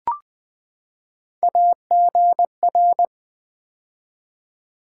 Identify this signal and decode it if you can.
Morse code, heard as a keyed tone, about 20 wpm, 700 Hz. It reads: AGR
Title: Build A Word - 3 Letters At A Time - 9-Letter Words - 20wpm